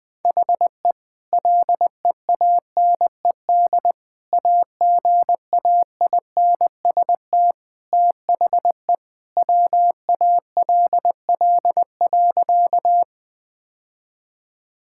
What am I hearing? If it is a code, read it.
Morse code, heard as a keyed tone, about 21 wpm, 700 Hz. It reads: HE LEANED AGAINST THE WALL.
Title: Sentences with 2-Word Phrases - 20wpm